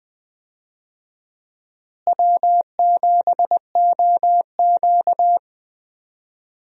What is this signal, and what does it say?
Morse code, heard as a keyed tone, about 20 wpm, 700 Hz. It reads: W7OQ
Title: USA Call Signs - 4 Characters - Repeated 3x - 20wpm